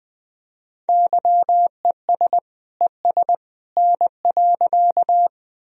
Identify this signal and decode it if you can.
Morse code, heard as a keyed tone, about 20 wpm, 700 Hz. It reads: YES ES N.
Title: Sentences from Top 500 Words and Abbreviations 20wpm